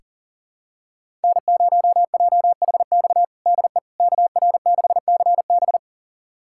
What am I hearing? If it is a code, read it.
Morse code, heard as a keyed tone, about 40 wpm, 700 Hz. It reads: N0JHX DE KR6CB
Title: USA Callsign DE Callsign - 5 Characters - 40wpm